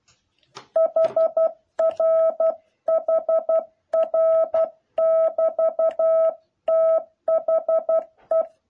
Morse code, heard as a keyed tone, about 12 wpm, 650 Hz. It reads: HR HR<BT>THE
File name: jh8cbh / 2021cw